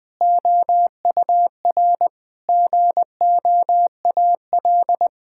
Morse code, heard as a keyed tone, about 21 wpm, 700 Hz. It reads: OUR GOAL
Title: Sentences from Top 1000 Words - Review Part 6 - 20wpm